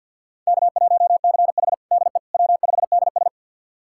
Morse code, heard as a keyed tone, about 45 wpm, 700 Hz. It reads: K1CH DE W5DS
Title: USA Callsign DE Callsign - 4 Characters - 50wpm